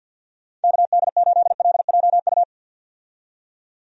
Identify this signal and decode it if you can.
Morse code, heard as a keyed tone, about 52 wpm, 700 Hz. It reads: KD8LJU
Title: USA Call Signs - 6 Characters - 50wpm